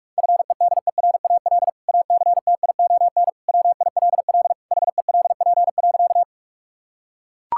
Morse code, heard as a keyed tone, about 43 wpm, 700 Hz. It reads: FEDERAL ACTION WILL HELP.
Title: Sentences from Top 1000 Words - Review Part 5 - 45wpm